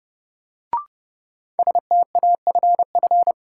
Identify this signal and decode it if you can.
Morse code, heard as a keyed tone, about 30 wpm, 700 Hz. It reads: STAFF
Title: Sets of 5 Words Encore - New Word Drill - 30wpm